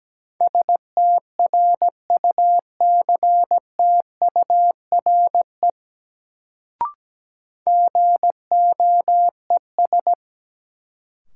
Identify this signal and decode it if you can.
Morse code, heard as a keyed tone, about 17 wpm, 700 Hz. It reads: STRUCTURE GOES
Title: Sentences from Top 1500 Words - New Word Drill Alt - 17wpm